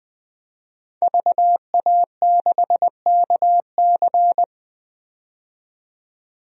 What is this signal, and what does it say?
Morse code, heard as a keyed tone, about 20 wpm, 700 Hz. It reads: VA6KC
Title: DXCC-100 Call Signs - 5 Characters - 20wpm